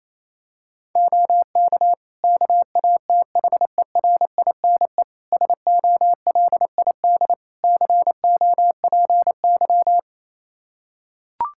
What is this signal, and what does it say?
Morse code, heard as a keyed tone, about 28 wpm, 700 Hz. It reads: OK KATHERINE SOLID COPY
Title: Practice QSO - Exchange 3 - More Info - 28wpm